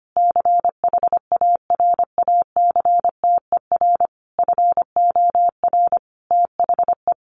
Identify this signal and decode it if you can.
Morse code, heard as a keyed tone, about 25 wpm, 700 Hz. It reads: CHARACTER FOR THE
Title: Sentences from Top 1200 Words 25wpm